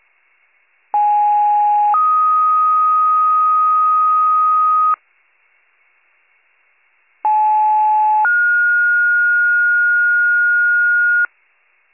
Two-tone page, heard as a one-second tone, then a three-second tone, about 830 then 1250 Hz, about 860 then 1430 Hz. It is heard from 0.9 to 11.3 s.